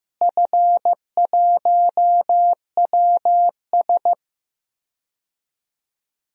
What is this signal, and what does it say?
Morse code, heard as a keyed tone, about 15 wpm, 700 Hz. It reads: F1WS
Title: DXCC-100 Call Signs - 4 Characters - Repeated 2x - 15wpm